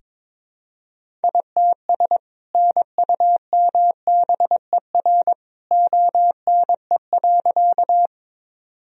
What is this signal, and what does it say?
Morse code, heard as a keyed tone, about 22 wpm, 700 Hz. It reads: ITS NUMBER ONE.
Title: Sentences from Top 1000 Words - Review Part 1 - 22wpm